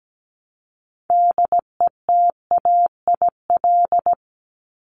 Morse code, heard as a keyed tone, about 17 wpm, 700 Hz. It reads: DETAIL
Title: Top 1251-1500 Words 17wpm